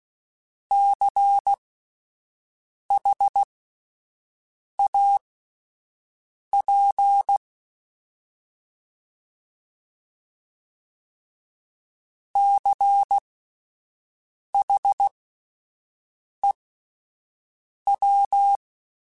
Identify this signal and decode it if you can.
Morse code, heard as a keyed tone, about 16 wpm, spaced as for 5 wpm, 800 Hz. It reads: CHAP CHEW
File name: Alphabetical / Alphabetical mp3s / Cs @5